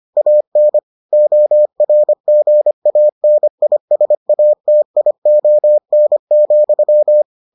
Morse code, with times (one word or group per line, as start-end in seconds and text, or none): 0.2-0.8 AN
1.1-7.2 ORGANISATION,